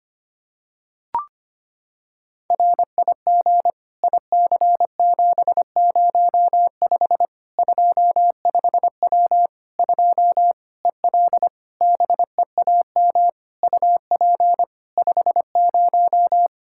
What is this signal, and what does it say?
Morse code, heard as a keyed tone, about 25 wpm, 700 Hz. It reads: RIG IC705 25W 2 EL BEAM UP 50
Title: Practice QSO - Exchange 2 - Info - 25wpm